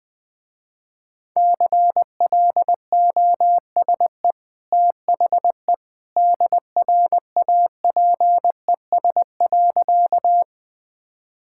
Morse code, heard as a keyed tone, about 21 wpm, 700 Hz. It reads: CLOSE THE DRAPES.
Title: Easy Sentences with Hints - 20wpm